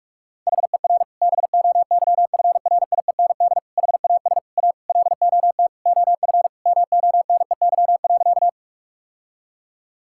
Morse code, heard as a keyed tone, about 46 wpm, 690 Hz. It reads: HER BOYFRIEND HAS A LOT OF MONEY.
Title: Sentences from Top 1000 Words - Review Part 5 - 45wpm